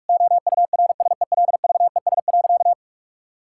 Morse code, heard as a keyed tone, about 42 wpm, 700 Hz. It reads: OURSELVES.
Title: Sentences from Top 1600 Words 45wpm